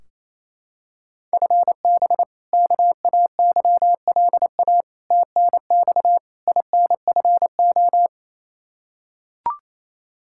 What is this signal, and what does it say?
Morse code, heard as a keyed tone, about 28 wpm, 700 Hz. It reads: FB KAYLA TNX INFO
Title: K1USN SST QSO Practice 28wpm